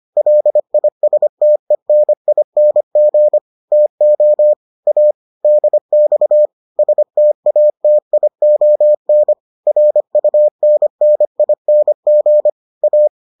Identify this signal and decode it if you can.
Morse code, heard as a keyed tone, about 25 wpm, 600 Hz. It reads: LISTENING TO A DX STATION RUNNING A